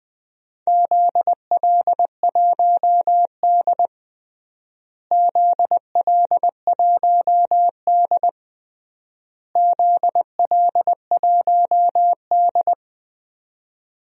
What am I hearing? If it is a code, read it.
Morse code, heard as a keyed tone, about 19 wpm, 700 Hz. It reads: ZL1D ZL1D ZL1D
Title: DXCC-100 Call Signs - 4 Characters - Repeated 3x - 20wpm